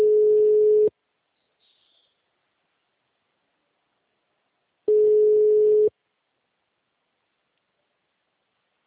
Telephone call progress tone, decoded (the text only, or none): ringback tone